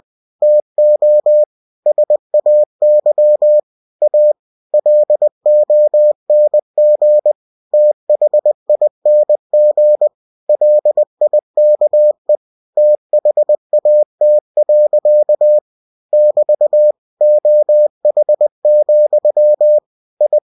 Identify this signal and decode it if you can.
Morse code, heard as a keyed tone, about 20 wpm, 600 Hz. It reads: TO SAY A LONG THING LIKE THAT. <BT> OH, I